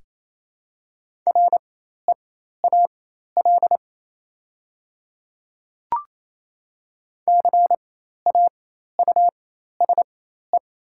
Morse code, heard as a keyed tone, about 28 wpm, spaced as for 13 wpm, 700 Hz. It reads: REAL CAUSE
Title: 4x Character Spacing - Top 300 Words - 28wpm